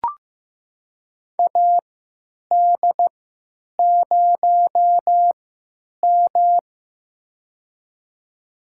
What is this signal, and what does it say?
Morse code, heard as a keyed tone, about 15 wpm, spaced as for 9 wpm, 700 Hz. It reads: AD0M
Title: USA Call Signs - 4 Characters - 3x Character Spacing - 15wpm